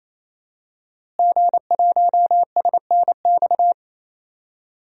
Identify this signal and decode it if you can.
Morse code, heard as a keyed tone, about 27 wpm, 700 Hz. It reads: G1SNX
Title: DXCC-100 Call Signs - 5 Characters - Repeated 3x - 28wpm